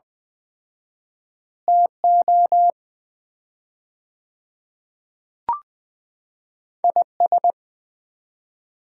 Morse code, heard as a keyed tone, about 20 wpm, 700 Hz. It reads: TO IS
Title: 4x Word-Spacing - Sets of 4 Words - 20wpm